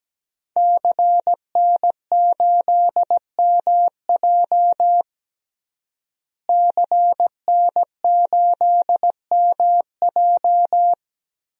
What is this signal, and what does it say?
Morse code, heard as a keyed tone, about 17 wpm, 700 Hz. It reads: CN8MJ CN8MJ
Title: DXCC-100 Call Signs - 5 Characters - Repeated 2x - 17wpm